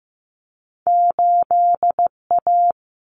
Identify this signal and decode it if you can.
Morse code, heard as a keyed tone, about 15 wpm, 700 Hz. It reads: 8A
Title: Common Cut Numbers - 2-Digit 15wpm